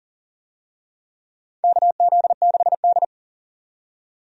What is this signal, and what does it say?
Morse code, heard as a keyed tone, about 39 wpm, 690 Hz. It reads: KZ6D
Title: USA Call Signs - 4 Characters - 40wpm